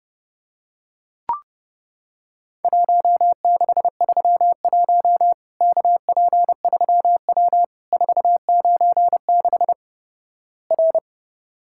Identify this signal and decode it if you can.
Morse code, heard as a keyed tone, about 30 wpm, 700 Hz. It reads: 1631 KP3W 496 R
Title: Worked All Europe DX Contest - QTC Exchange - 30wpm